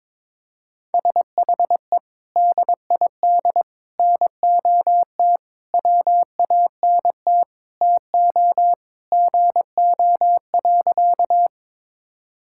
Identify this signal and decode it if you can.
Morse code, heard as a keyed tone, about 22 wpm, 700 Hz. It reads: SHE DID NOT WANT TO GO.
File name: practice / 22wpm / Sentences with 4-Word Phrases - 22wpm